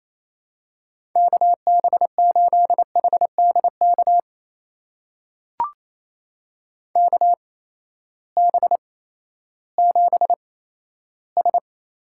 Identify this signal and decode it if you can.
Morse code, heard as a keyed tone, about 28 wpm, spaced as for 14 wpm, 700 Hz. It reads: KB8HDK KB7S
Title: USA Call Signs - 8x Character Spacing - 28wpm